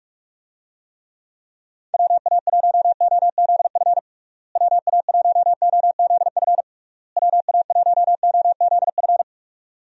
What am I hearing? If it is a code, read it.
Morse code, heard as a keyed tone, about 43 wpm, 700 Hz. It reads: WA1OZF WA1OZF WA1OZF
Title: USA Call Signs - 6 Characters - Repeated 3x - 45wpm